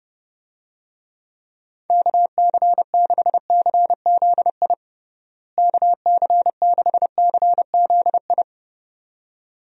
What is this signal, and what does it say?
Morse code, heard as a keyed tone, about 30 wpm, 700 Hz. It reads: KC6CZI KC6CZI
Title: USA Call Signs - 6 Characters - Repeated 2x - 30wpm